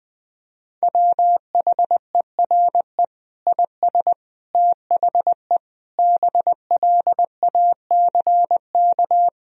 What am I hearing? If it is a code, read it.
Morse code, heard as a keyed tone, about 21 wpm, 700 Hz. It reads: WHERE IS THE BLACK
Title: Sentences from Top 1000 Words - Review Part 2 - 20wpm